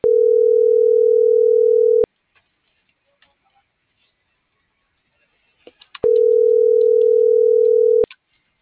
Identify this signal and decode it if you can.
Ringback tone (North America), 440 + 480 Hz.